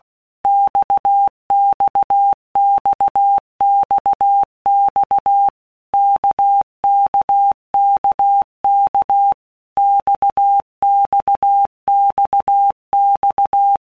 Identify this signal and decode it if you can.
Morse code, heard as a keyed tone, about 16 wpm, 800 Hz. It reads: XXXXX KKKK XXXX
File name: Basic Letter Groups By Blocks / Groups@15 / RKPX@15